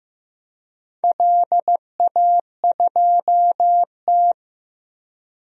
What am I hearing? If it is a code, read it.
Morse code, heard as a keyed tone, about 15 wpm, 700 Hz. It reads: LA2T
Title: DXCC-100 Call Signs - 4 Characters - 15wpm